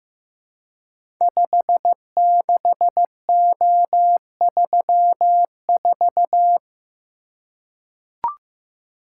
Morse code, heard as a keyed tone, about 15 wpm, 700 Hz. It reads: 56O34
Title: Code Groups - 5 Letter:Numbers - 15wpm